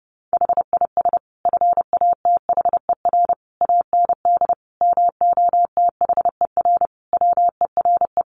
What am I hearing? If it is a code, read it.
Morse code, heard as a keyed tone, about 30 wpm, 700 Hz. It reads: HIS FATHER AND MOTHER WERE
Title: Sentences from Top 1500 Words 30wpm